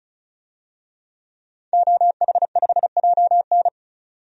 Morse code, heard as a keyed tone, about 36 wpm, 700 Hz. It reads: OH5JN
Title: DXCC-100 Call Signs - 5 Characters - 35wpm